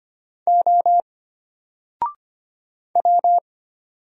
Morse code, heard as a keyed tone, about 25 wpm, 700 Hz. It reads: O W
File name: Single-Letter-Number-25wpm